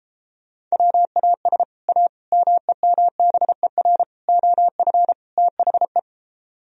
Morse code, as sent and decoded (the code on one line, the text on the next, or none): .-- .- ... / .- / -- . -- -... .-. / --- ..-. / - .... .
WAS A MEMBER OF THE